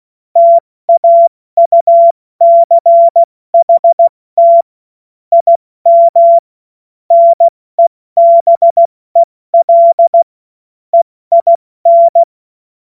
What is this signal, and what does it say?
Morse code, heard as a keyed tone, about 16 wpm, 680 Hz. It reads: TAUCHT IM NEBEL EIN
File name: Morsekurs / Witze / Witz 03. Mai 2022 70 F80